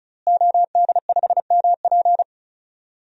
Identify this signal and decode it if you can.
Morse code, heard as a keyed tone, about 34 wpm, 700 Hz. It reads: OD5MP
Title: DXCC-100 Call Signs - 5 Characters - Repeated 2x - 35wpm